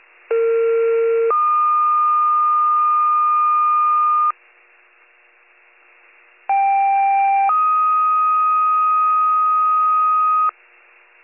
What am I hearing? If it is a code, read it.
Two-tone page, heard as a one-second tone, then a three-second tone, about 460 then 1220 Hz, about 790 then 1250 Hz. It is heard from 0.3 to 10.5 s.